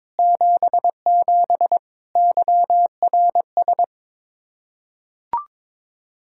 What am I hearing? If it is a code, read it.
Morse code, heard as a keyed tone, about 21 wpm, 700 Hz. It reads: 77 YRS
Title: Practice QSO - Exchange 3 - More Info - 22wpm